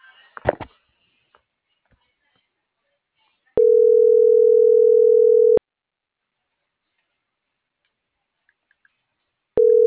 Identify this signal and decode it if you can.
Ringback tone (North America), 440 + 480 Hz.